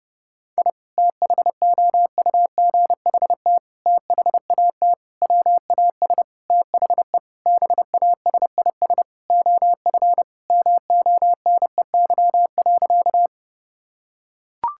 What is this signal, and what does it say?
Morse code, heard as a keyed tone, about 28 wpm, 700 Hz. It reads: I THOUGHT THAT WAS THE BASIS OF MONEY.
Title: Sentences from Top 1000 Words - Review Part 5 - 30wpm